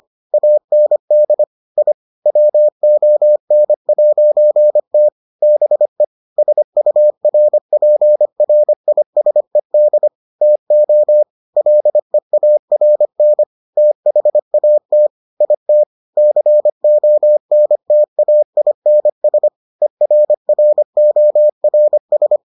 Morse code, heard as a keyed tone, about 25 wpm, 600 Hz. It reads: AND I WON'T BE SURPRISED TO LEARN THAT IT CONTAINS ERRORS